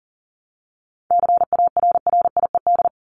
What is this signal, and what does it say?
Morse code, heard as a keyed tone, about 40 wpm, 700 Hz. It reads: CARRIED
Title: ED Suffix - Top 100 Words with Base Words 40wpm